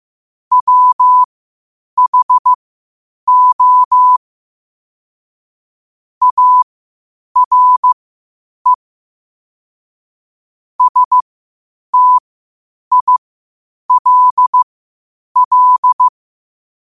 Morse code, heard as a keyed tone, about 15 wpm, spaced as for 8 wpm, 1000 Hz. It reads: WHO ARE STILL
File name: qotdfiles / 8wpm